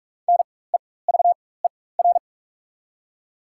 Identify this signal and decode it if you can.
Morse code, heard as a keyed tone, about 43 wpm, spaced as for 21 wpm, 700 Hz. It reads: NEVER
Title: 4x Character Spacing - Top 500 Words - 45wpm